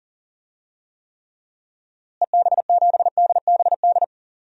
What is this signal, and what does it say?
Morse code, heard as a keyed tone, about 40 wpm, 700 Hz. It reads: EB7DBD
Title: DXCC-100 Call Signs - 6 Characters - Repeated 3x - 40wpm